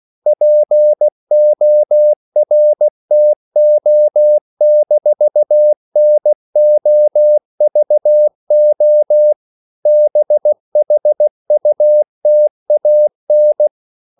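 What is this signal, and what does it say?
Morse code, heard as a keyed tone, about 16 wpm, 600 Hz. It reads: PORTO-NOVO BHUTAN
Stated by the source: CW practice of national capitals at 16 words per minute, from ON6ZQ.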